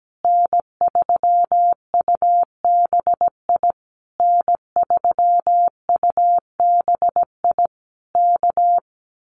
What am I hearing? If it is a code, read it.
Morse code, heard as a keyed tone, about 17 wpm, 700 Hz. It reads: N3UBI N3UBI K
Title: Common QSO Phrases Encore 17wpm